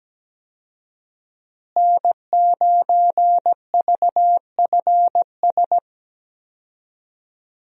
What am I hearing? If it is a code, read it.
Morse code, heard as a keyed tone, about 17 wpm, 700 Hz. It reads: N9VFS
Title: USA Call Signs - 5 Characters - Repeated 2x - 17wpm